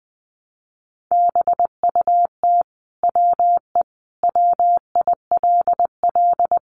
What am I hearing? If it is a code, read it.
Morse code, heard as a keyed tone, about 20 wpm, 700 Hz. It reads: BUT WE WILL
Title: Sentences from Top 1200 Words 20wpm